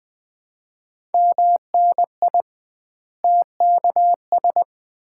Morse code, heard as a keyed tone, about 20 wpm, 700 Hz. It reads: MNI TKS
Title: Common QSO Phrases - 2x Word-Spacing - 20wpm